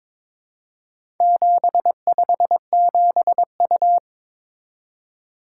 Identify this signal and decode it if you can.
Morse code, heard as a keyed tone, about 23 wpm, 700 Hz. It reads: 757U
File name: practice / 22wpm / Four Letter-Number Combinations - 22wpm